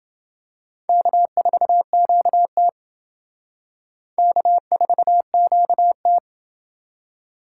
Morse code, heard as a keyed tone, about 28 wpm, 700 Hz. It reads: K4QT K4QT